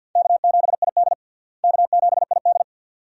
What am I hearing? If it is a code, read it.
Morse code, heard as a keyed tone, about 52 wpm, 700 Hz. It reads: K7ID K7ID
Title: USA Call Signs - 4 Characters - Repeated 2x - 50wpm